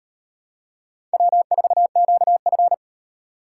Morse code, heard as a keyed tone, about 38 wpm, 700 Hz. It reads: W4QF